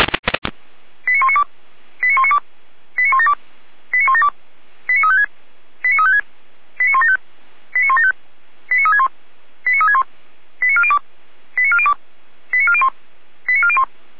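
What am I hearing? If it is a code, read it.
Five-tone selective call (ZVEI), heard as quick runs of five tones, tones of about 70 ms. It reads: 89192 89192 89172 89172 89356 89356 89175 89175 89361 89361 89402 89402 89401 89401
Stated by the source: ZVEI five-tone call received by SDR.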